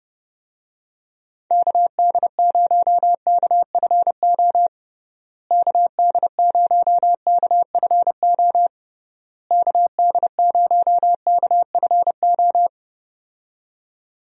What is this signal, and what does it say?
Morse code, heard as a keyed tone, about 30 wpm, 700 Hz. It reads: KD0KFO KD0KFO KD0KFO
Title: USA Call Signs - 6 Characters - Repeated 3x - 30wpm